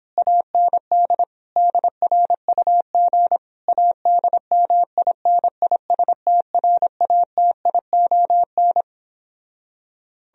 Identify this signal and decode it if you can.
Morse code, heard as a keyed tone, about 26 wpm, 700 Hz. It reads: AND DRUG ADMINISTRATION
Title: Speed-Racing - Sets of 5 Words 22wpm